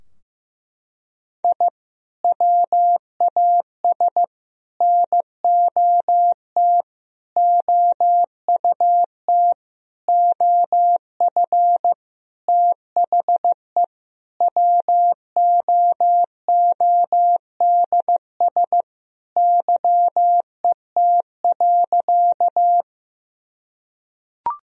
Morse code, heard as a keyed tone, about 15 wpm, 700 Hz. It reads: I WAS NOT OUT OF THE WOODS YET.
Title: Sentences from Top 1000 Words - Review Part 2 - 15wpm